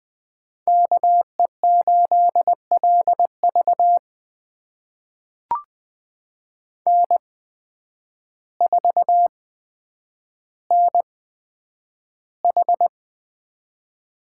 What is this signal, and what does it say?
Morse code, heard as a keyed tone, about 20 wpm, spaced as for 9 wpm, 700 Hz. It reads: KE8LV N4NH
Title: USA Call Signs - 5 Characters - 8x Character Spacing - 20wpm